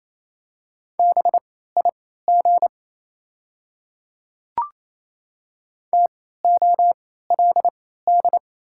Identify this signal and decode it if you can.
Morse code, heard as a keyed tone, about 27 wpm, spaced as for 16 wpm, 700 Hz. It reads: BIG TOLD
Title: 3x Character Spacing - Top 500 Words - 28wpm